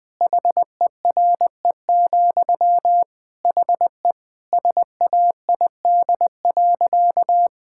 Morse code, heard as a keyed tone, about 20 wpm, 700 Hz. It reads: HERE, HE SAID.
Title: Sentences from Top 1600 Words 20wpm